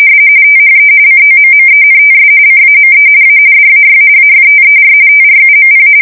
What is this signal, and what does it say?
RTTY, about 45.45 baud. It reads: THE COMMENTS CAME A DAY AFTER TRUMP
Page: 170 Shift 2125/2295